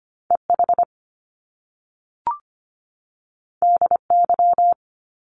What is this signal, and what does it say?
Morse code, heard as a keyed tone, about 25 wpm, 700 Hz. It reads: EH DY
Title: Top Two Letter Combinations in Words 25wpm